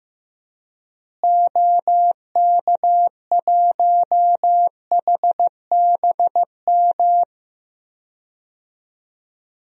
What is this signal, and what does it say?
Morse code, heard as a keyed tone, about 15 wpm, 700 Hz. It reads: OK1HBM
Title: DXCC-100 Call Signs - 6 Characters - Repeated 2x - 15wpm